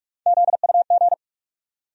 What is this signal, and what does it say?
Morse code, heard as a keyed tone, about 41 wpm, 700 Hz. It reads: ZUG
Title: USA Call Sign Postfixes - 3 Letters 45wpm